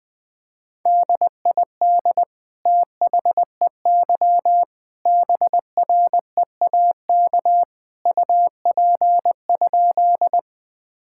Morse code, heard as a keyed tone, about 20 wpm, 700 Hz. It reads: DID THEY BREAK UP?
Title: Sentences from Top 1000 Words - Review Part 6 - 20wpm